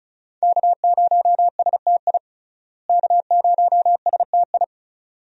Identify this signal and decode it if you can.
Morse code, heard as a keyed tone, about 36 wpm, 700 Hz. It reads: K0STI K0STI
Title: USA Call Signs - 5 Characters - Repeated 2x - 35wpm